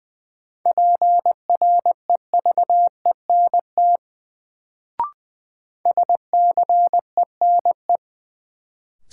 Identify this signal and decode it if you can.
Morse code, heard as a keyed tone, about 20 wpm, 700 Hz. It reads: PREVENT SCENE
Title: Sentences from Top 1100 Words - New Word Drill Alt - 20wpm